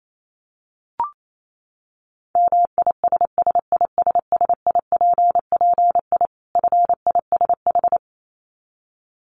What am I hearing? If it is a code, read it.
Morse code, heard as a keyed tone, about 28 wpm, 700 Hz. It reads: MISSISSIPPI FISH
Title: Dit Challenge 28wpm